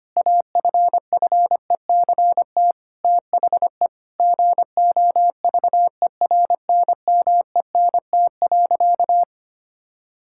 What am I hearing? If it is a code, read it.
Morse code, heard as a keyed tone, about 25 wpm, 700 Hz. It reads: AFFECT THE GOVERNMENT.
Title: Sentences from Top 1000 Words - Review Part 6 - 25wpm